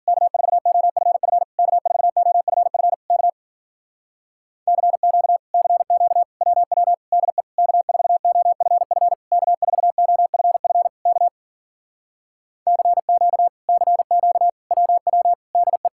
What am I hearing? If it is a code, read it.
Morse code, heard as a keyed tone, about 53 wpm, 700 Hz. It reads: K4OFF K4OFF K CQ CQ WW DE K4OFF K4OFF K CQ CQ WW DE